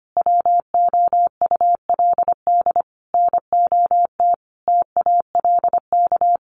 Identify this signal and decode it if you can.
Morse code, heard as a keyed tone, about 25 wpm, 700 Hz. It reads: WOULD NOT TALK
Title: Sentences from Top 800 Words 25wpm